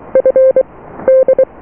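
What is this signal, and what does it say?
Morse code, heard as a keyed tone, about 23 wpm, 520 Hz. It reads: F D
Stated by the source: Morse QSO on 14 MHz with a German station, received at RST 579.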